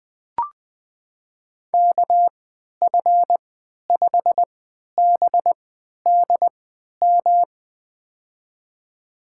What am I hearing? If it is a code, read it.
Morse code, heard as a keyed tone, about 20 wpm, spaced as for 11 wpm, 700 Hz. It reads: KF5BDM
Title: USA Call Signs - 6 Characters - 3x Character Spacing - 20wpm